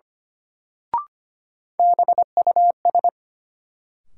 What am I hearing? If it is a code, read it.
Morse code, heard as a keyed tone, about 25 wpm, 700 Hz. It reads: BUS
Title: Sentences from Top 1000 Words - Review Part 6 - New Word Drill - 25wpm